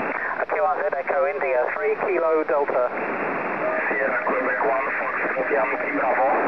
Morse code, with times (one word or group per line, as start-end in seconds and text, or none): none